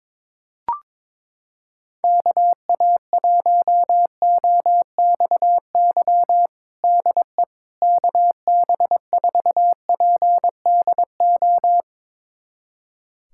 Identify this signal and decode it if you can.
Morse code, heard as a keyed tone, about 21 wpm, 700 Hz. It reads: KA1OXY DE KB4PDO
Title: USA Callsign DE Callsign - 6 Characters - 22wpm